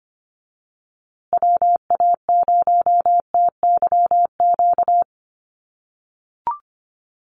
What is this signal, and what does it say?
Morse code, heard as a keyed tone, about 25 wpm, 700 Hz. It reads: WA0TYQ
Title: USA Call Signs Repeated 2x 25wpm